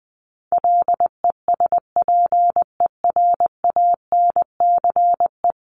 Morse code, heard as a keyed tone, about 20 wpm, 700 Hz. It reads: LESPERANCE
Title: A Sound Of Thunder 20wpm